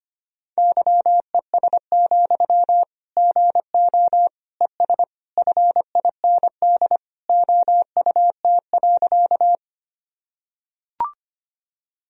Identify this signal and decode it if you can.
Morse code, heard as a keyed tone, about 25 wpm, 700 Hz. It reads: YES, GO ES FIND OUT.
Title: Sentences from Top 500 Words and Abbreviations 25wpm